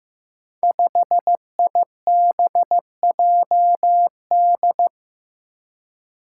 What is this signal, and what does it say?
Morse code, heard as a keyed tone, about 15 wpm, 700 Hz. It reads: 5IBJD
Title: Five Challenging Letter-Number Combinations 15wpm